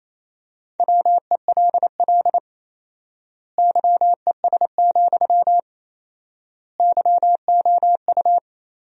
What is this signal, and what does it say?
Morse code, heard as a keyed tone, about 28 wpm, 700 Hz. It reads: WELL YES, YOU
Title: 4x Word-Spacing - Sentences from Top 500 Words - 28wpm